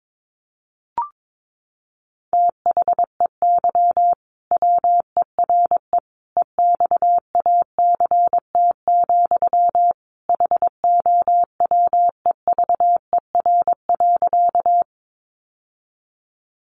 Morse code, heard as a keyed tone, about 22 wpm, 700 Hz. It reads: THEY WERE EXACT, HOWEVER.
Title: Sentences from Top 1300 Words 22wpm